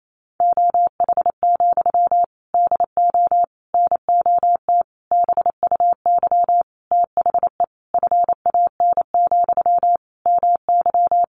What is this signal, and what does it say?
Morse code, heard as a keyed tone, about 28 wpm, 700 Hz. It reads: OH, DO NOT BUY THE FAN, MY
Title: Sentences from Top 1300 Words 28wpm